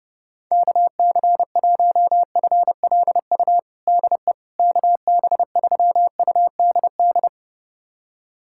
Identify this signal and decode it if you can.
Morse code, heard as a keyed tone, about 30 wpm, 700 Hz. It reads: KC1FLU DE KB3UDD
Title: USA Callsign DE Callsign - 6 Characters - 30wpm